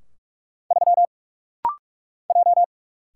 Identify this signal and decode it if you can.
Morse code, heard as a keyed tone, about 44 wpm, 700 Hz. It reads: 3 J
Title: Single Letter-Number Alt - Rapid-Fire 45wpm